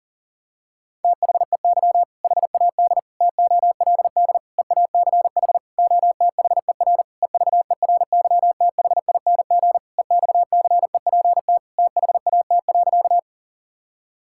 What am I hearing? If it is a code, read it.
Morse code, heard as a keyed tone, about 40 wpm, 700 Hz. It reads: THEY HAD TOLD EACH OTHER EVERYTHING EXCEPT THAT.
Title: Sentences from Top 1600 Words 40wpm